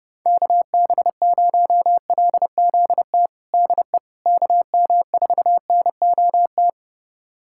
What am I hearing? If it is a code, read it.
Morse code, heard as a keyed tone, about 32 wpm, 700 Hz. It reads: KB0LZT DE KM4NOT
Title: USA Callsign DE Callsign - 6 Characters - 30wpm